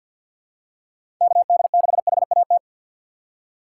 Morse code, heard as a keyed tone, about 49 wpm, 700 Hz. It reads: KD6HAT
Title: USA Call Signs - 2x Character Spacing - 50wpm